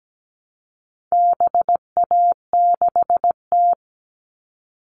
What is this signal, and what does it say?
Morse code, heard as a keyed tone, about 17 wpm, 700 Hz. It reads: BA6T
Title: All Cut Numbers - 4-Digit 17wpm